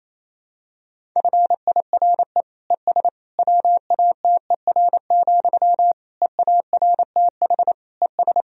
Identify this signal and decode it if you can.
Morse code, heard as a keyed tone, about 28 wpm, 700 Hz. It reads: FIRE ES WATER, EARTH ES
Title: Sentences from Top 400 Words and Abbreviations 28wpm